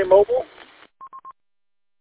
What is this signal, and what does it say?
Speech. There is no code.